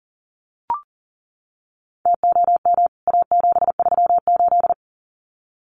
Morse code, heard as a keyed tone, about 37 wpm, 700 Hz. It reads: TOM A738
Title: CWT Exchange 40wpm